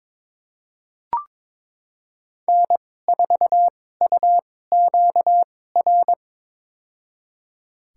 Morse code, heard as a keyed tone, about 21 wpm, spaced as for 16 wpm, 700 Hz. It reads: N4UQR
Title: USA Call Signs - 5 Characters - 2x Character Spacing - 22wpm